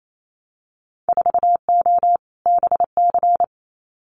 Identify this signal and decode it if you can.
Morse code, heard as a keyed tone, about 28 wpm, 700 Hz. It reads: 4O BC
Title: Winter Field Day Exchanges Encore 28wpm